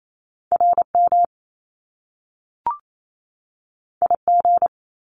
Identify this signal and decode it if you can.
Morse code, heard as a keyed tone, about 28 wpm, 700 Hz. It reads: RM IG
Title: Top Two Letter Combinations in Words 28wpm